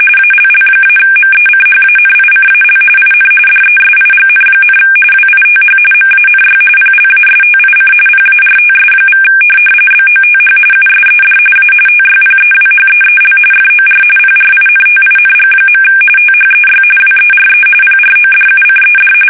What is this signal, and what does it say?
RTTY, about 45.45 baud. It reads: ALL-MECHANICAL IN DESIGN, WITH MANY PLASTIC PARTS= BOTH USED A TYPEWHEEL FOR PRINTING. THEY WERE PRODUCED
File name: fsk850 / teletype-corp